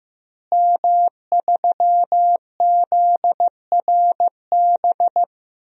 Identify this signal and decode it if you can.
Morse code, heard as a keyed tone, about 16 wpm, 700 Hz. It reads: M3ZRB
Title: DXCC-100 Call Signs - 5 Characters - Repeated 3x - 15wpm